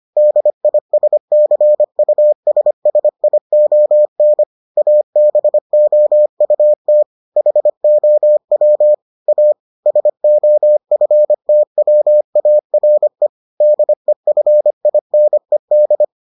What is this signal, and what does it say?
Morse code, heard as a keyed tone, about 25 wpm, 600 Hz. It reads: DISCUSSION ABOUT HOW A SOFTWARE DEFINED